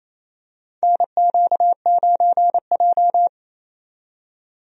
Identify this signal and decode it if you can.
Morse code, heard as a keyed tone, about 28 wpm, 700 Hz. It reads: NQ9J
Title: USA Call Signs - 4 Characters - 28wpm